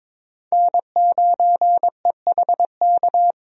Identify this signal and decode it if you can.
Morse code, heard as a keyed tone, about 22 wpm, 700 Hz. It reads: N9EHK
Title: USA Call Signs - 5 Characters - Repeated 3x - 22wpm